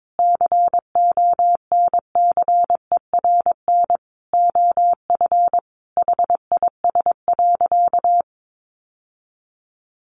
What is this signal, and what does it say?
Morse code, heard as a keyed tone, about 22 wpm, 700 Hz. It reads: CONCERN OF HIS.
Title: Sentences from Top 1000 Words 22wpm